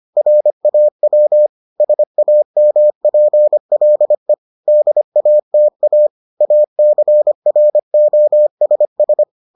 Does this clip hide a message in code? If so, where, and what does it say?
Yes: Morse code from 0.2 to 9.2 s, heard as a keyed tone, about 25 wpm, 600 Hz. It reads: RAW SAMPLE DATA ACROSS